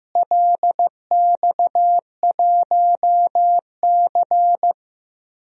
Morse code, heard as a keyed tone, about 15 wpm, 700 Hz. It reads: LX1C
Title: DXCC-100 Call Signs - 4 Characters - Repeated 2x - 15wpm